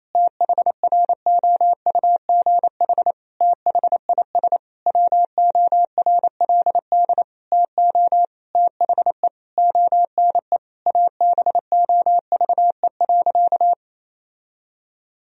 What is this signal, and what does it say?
Morse code, heard as a keyed tone, about 28 wpm, 700 Hz. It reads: THROUGH THIS WORLD TO THE ONE ABOVE.
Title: Sentences from Top 1000 Words - Review Part 2 - 28wpm